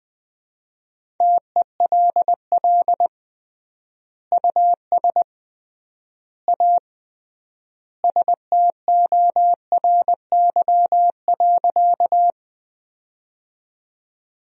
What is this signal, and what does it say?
Morse code, heard as a keyed tone, about 20 wpm, 700 Hz. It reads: TELL US A STORY.
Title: 3x Word-Spacing - Sentences from Top 300 Words - 20wpm